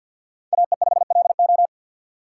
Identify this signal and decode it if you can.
Morse code, heard as a keyed tone, about 52 wpm, 700 Hz. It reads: AE5LO
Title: USA Call Signs - 3x Character Spacing - 50wpm